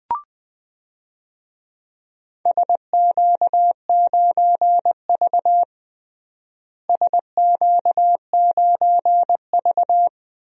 Morse code, heard as a keyed tone, about 21 wpm, 700 Hz. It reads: SQ9V SQ9V
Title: DXCC-100 Call Signs - 4 Characters - Repeated 2x - 20wpm